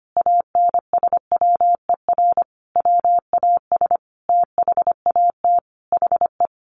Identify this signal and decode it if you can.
Morse code, heard as a keyed tone, about 25 wpm, 700 Hz. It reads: ANSWER WAS THAT HE
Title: Sentences from Top 1500 Words 25wpm